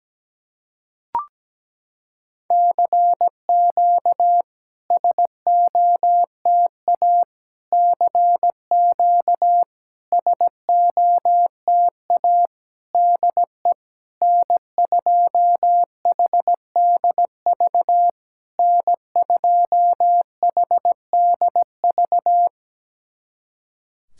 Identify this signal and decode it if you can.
Morse code, heard as a keyed tone, about 17 wpm, 700 Hz. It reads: CQ SOTA CQ SOTA DE N2HDV N2HDV
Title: SOTA QSO Practice 17wpm